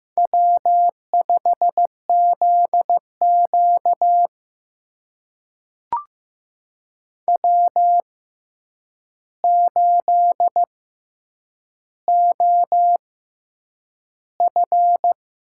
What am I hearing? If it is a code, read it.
Morse code, heard as a keyed tone, about 15 wpm, spaced as for 8 wpm, 700 Hz. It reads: W5ZQ W8OF
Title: USA Call Signs - 4 Characters - 6x Character Spacing - 15wpm